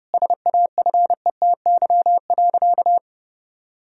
Morse code, heard as a keyed tone, about 30 wpm, 700 Hz. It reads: SAFETY.
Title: Sentences from Top 1000 Words - Review Part 4 - 30wpm